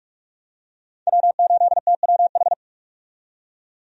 Morse code, heard as a keyed tone, about 44 wpm, 700 Hz. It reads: W8TWH
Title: USA Call Signs - 5 Characters - 45wpm